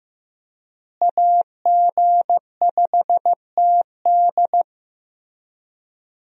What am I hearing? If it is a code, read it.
Morse code, heard as a keyed tone, about 15 wpm, 700 Hz. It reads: AG5TD
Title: USA Call Signs - 5 Characters - Repeated 2x - 15wpm